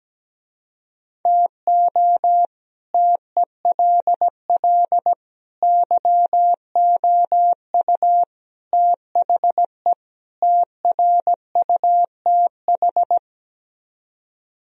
Morse code, heard as a keyed tone, about 17 wpm, 700 Hz. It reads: TO TELL YOU THE TRUTH
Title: Speed-Racing - Sets of 5 Words 17wpm